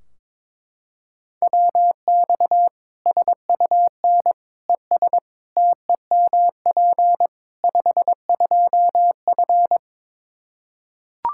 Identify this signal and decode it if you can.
Morse code, heard as a keyed tone, about 22 wpm, 700 Hz. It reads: WX SUN ES TEMP 52F
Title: Practice QSO - Exchange 2 - Info - 22wpm